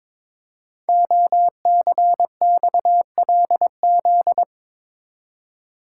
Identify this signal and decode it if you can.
Morse code, heard as a keyed tone, about 22 wpm, 700 Hz. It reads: OCXLZ
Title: Code Groups - 5 Letters - 22wpm